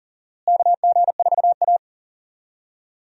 Morse code, heard as a keyed tone, about 37 wpm, 700 Hz. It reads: KG4A
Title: USA Call Signs - 4 Characters - Repeated 2x - 40wpm